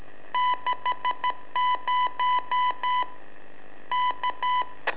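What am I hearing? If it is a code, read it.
Morse code, heard as a keyed tone, about 13 wpm, 1000 Hz. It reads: 60 K
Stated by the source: Telemetry in Morse from LU7AA and LU4EZ, AMSAT and Turma Mar del Plata, sent from a balloon.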